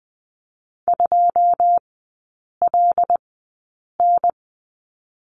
Morse code, heard as a keyed tone, about 20 wpm, spaced as for 8 wpm, 700 Hz. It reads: 2LN
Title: 081 Lesson 14e - Number 2 - Three Characters 20wpm